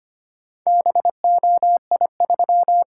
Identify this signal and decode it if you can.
Morse code, heard as a keyed tone, about 25 wpm, 700 Hz. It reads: BOI3
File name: practice / 25wpm / Four Letter-Number Combinations - 25wpm